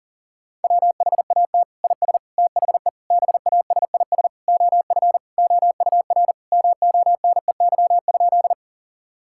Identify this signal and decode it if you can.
Morse code, heard as a keyed tone, about 40 wpm, 700 Hz. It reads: WHAT IS THE BASIS OF OUR MONEY?
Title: Sentences from Top 1000 Words - Review Part 5 - 40wpm